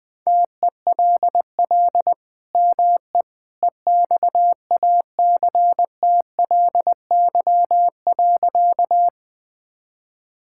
Morse code, heard as a keyed tone, about 20 wpm, 700 Hz. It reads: TELL ME EXACTLY.
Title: Sentences from Top 1000 Words - Review Part 6 - 20wpm